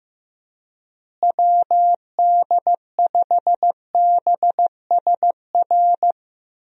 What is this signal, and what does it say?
Morse code, heard as a keyed tone, about 15 wpm, 700 Hz. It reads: WD5BSR
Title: USA Call Signs - 6 Characters - Repeated 3x - 15wpm